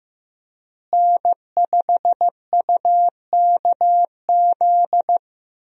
Morse code, heard as a keyed tone, about 15 wpm, 700 Hz. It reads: N5UKZ
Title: USA Call Signs - 5 Characters - Repeated 2x - 15wpm